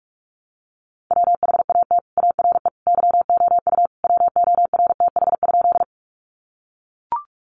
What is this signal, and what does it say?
Morse code, heard as a keyed tone, about 44 wpm, 700 Hz. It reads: WHAT ARE YOU WORTH?
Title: Sentences from Top 800 Words 45wpm